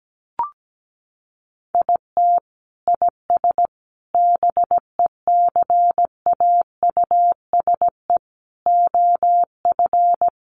Morse code, heard as a keyed tone, about 17 wpm, 700 Hz. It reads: IT IS BECAUSE OF
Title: Sentences from Top 600 Words 17wpm